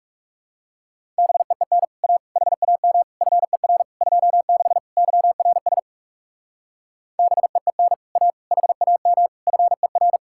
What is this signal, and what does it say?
Morse code, heard as a keyed tone, about 44 wpm, 700 Hz. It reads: BEEN A HAM FER 26 YRS BEEN A HAM FER